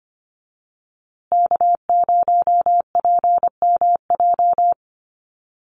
Morse code, heard as a keyed tone, about 25 wpm, 700 Hz. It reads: K0PMJ
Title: USA Call Signs Repeated 3x 25wpm